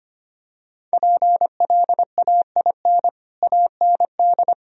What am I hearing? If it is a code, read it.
Morse code, heard as a keyed tone, about 25 wpm, 700 Hz. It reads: PLAIN AND
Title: Sentences from Top 1000 Words - Review Part 2 - 25wpm